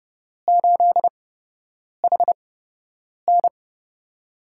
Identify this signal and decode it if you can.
Morse code, heard as a keyed tone, about 30 wpm, spaced as for 8 wpm, 700 Hz. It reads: 8HN
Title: USA Call Signs - 5 Characters - 8x Character Spacing - 30wpm